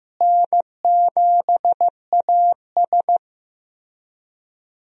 Morse code, heard as a keyed tone, about 15 wpm, 700 Hz. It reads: N7AS
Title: USA Call Signs - 4 Characters - Repeated 3x - 15wpm